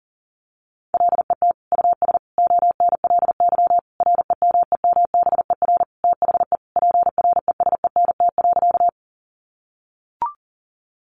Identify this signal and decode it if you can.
Morse code, heard as a keyed tone, about 40 wpm, 700 Hz. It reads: LET US ONLY REMEMBER THE PRESENT.
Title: Sentences from Top 800 Words 40wpm